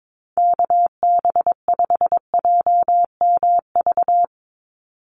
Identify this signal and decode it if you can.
Morse code, heard as a keyed tone, about 22 wpm, 700 Hz. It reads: KB5JMV
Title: USA Call Signs 22wpm